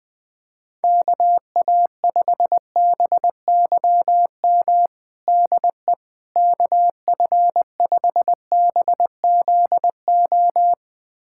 Morse code, heard as a keyed tone, about 20 wpm, 700 Hz. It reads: KA5BYM DE KF5BZO
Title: USA Callsign DE Callsign 20wpm